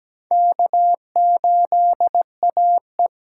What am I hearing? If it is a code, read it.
Morse code, heard as a keyed tone, about 17 wpm, 700 Hz. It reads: K8AE
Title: USA Call Signs - 8x Character Spacing - 17wpm